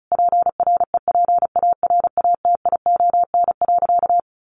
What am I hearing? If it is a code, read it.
Morse code, heard as a keyed tone, about 34 wpm, 700 Hz. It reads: PREPARATION.
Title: Sentences from Top 1400 Words 35wpm